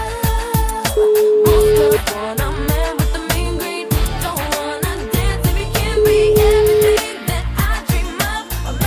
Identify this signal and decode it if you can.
Ringback tone (Europe, 425 Hz), with noise.